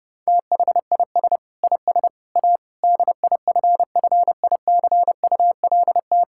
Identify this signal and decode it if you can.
Morse code, heard as a keyed tone, about 32 wpm, 700 Hz. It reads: THIS IS A DIFFICULT
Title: Sentences from Top 1000 Words - Review Part 5 - 30wpm